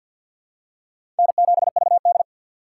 Morse code, heard as a keyed tone, about 50 wpm, 700 Hz. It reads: N7VD
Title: USA Call Signs - 2x Character Spacing - 50wpm